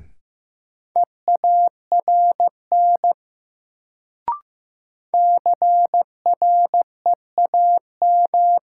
Morse code, heard as a keyed tone, about 15 wpm, 700 Hz. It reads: EARN CREAM
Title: Sentences from Top 1500 Words - New Word Drill Alt - 15wpm